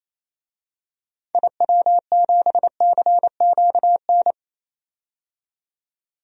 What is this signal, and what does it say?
Morse code, heard as a keyed tone, about 28 wpm, 700 Hz. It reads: IW7CQN
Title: DXCC-100 Call Signs - 6 Characters - 28wpm